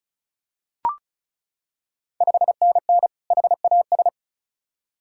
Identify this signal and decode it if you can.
Morse code, heard as a keyed tone, about 36 wpm, 690 Hz. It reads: 5NN HAS
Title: Ontario QSO Party Exchange 35wpm